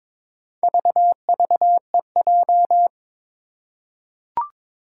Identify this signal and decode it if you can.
Morse code, heard as a keyed tone, about 22 wpm, 700 Hz. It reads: VVEJ
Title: Four Challenging Letter-Number Combinations 22wpm